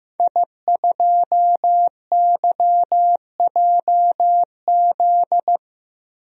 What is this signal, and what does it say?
Morse code, heard as a keyed tone, about 15 wpm, 700 Hz. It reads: I2YJZ
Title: DXCC-100 Call Signs - 5 Characters - Repeated 3x - 15wpm